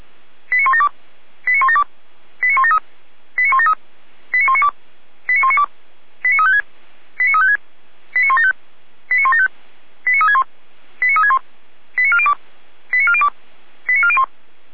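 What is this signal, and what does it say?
Five-tone selective call (ZVEI), heard as quick runs of five tones, tones of about 70 ms. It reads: 89172 89172 89173 89173 89192 89192 89356 89356 89175 89175 89361 89361 89402 89402 89401
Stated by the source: ZVEI five-tone selective call 89338 received by SDR.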